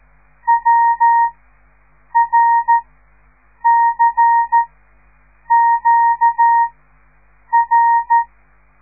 Morse code, heard as a keyed tone, about 13 wpm, spaced as for 7 wpm, 930 Hz. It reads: WRCQR
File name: radiomorningstar / morse / les 5